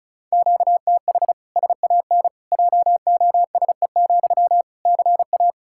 Morse code, heard as a keyed tone, about 36 wpm, 700 Hz. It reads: QTH SAN JOSE, CA
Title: Practice QSO - Exchange 1 - Beginning - 35wpm